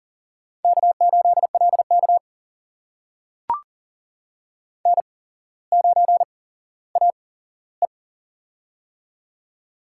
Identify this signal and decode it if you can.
Morse code, heard as a keyed tone, about 42 wpm, spaced as for 17 wpm, 700 Hz. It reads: K8LK N9AE
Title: USA Call Signs - 4 Characters - 8x Character Spacing - 40wpm